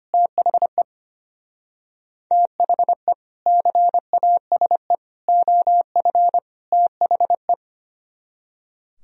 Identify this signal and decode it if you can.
Morse code, heard as a keyed tone, about 30 wpm, 700 Hz. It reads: THE THE CASE OF THE
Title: Speed-Racing - Sets of 4 Words 25wpm